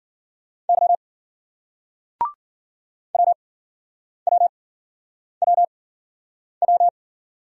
Morse code, heard as a keyed tone, about 50 wpm, spaced as for 9 wpm, 700 Hz. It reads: Y WWWW
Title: Speed-Racing Single Letter-Number 40wpm